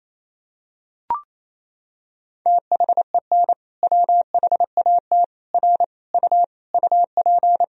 Morse code, heard as a keyed tone, about 28 wpm, 700 Hz. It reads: THEN WHAT R U UP
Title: Sentences from Top 100 Words and Abbreviations 28wpm